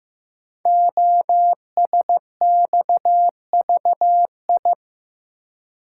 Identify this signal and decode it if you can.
Morse code, heard as a keyed tone, about 15 wpm, 700 Hz. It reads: OSXVI
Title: Code Groups - 5 Letters - 15wpm